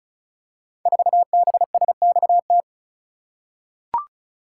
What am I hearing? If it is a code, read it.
Morse code, heard as a keyed tone, about 36 wpm, 700 Hz. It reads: 4BSXT
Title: Code Groups - 5 Letter:Numbers - 35wpm